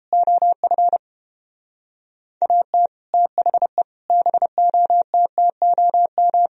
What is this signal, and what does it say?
Morse code, heard as a keyed tone, about 28 wpm, 700 Hz. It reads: OF AT THE BOTTOM